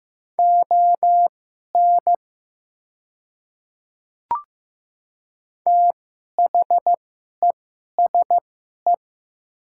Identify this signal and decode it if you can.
Morse code, heard as a keyed tone, about 15 wpm, spaced as for 11 wpm, 700 Hz. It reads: ON THESE
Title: 2x Character Spacing - Top 100 Words - 15wpm